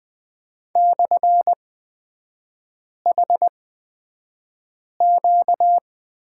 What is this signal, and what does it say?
Morse code, heard as a keyed tone, about 20 wpm, spaced as for 5 wpm, 700 Hz. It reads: /HQ
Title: 236 Lesson 38e - Letter q - Three Characters 20wpm